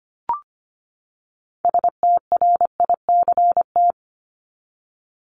Morse code, heard as a keyed tone, about 25 wpm, 700 Hz. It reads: STRICT
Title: Top 1501-1750 Words 25wpm